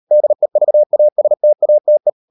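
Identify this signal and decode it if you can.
Morse code, heard as a keyed tone, about 40 wpm, 600 Hz. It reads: DEVASTATE
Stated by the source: Morse code practice at 35–39 wpm sending English text of Ken Follett's novel Never, from CW med Gnister.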